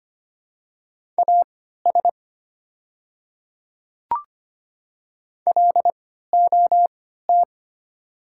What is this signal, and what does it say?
Morse code, heard as a keyed tone, about 25 wpm, spaced as for 14 wpm, 700 Hz. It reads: AS LOT
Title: 3x Character Spacing - Top 500 Words - 25wpm